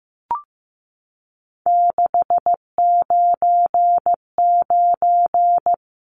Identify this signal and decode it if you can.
Morse code, heard as a keyed tone, about 15 wpm, 700 Hz. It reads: 699
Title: Numbers - 3-Digit 15wpm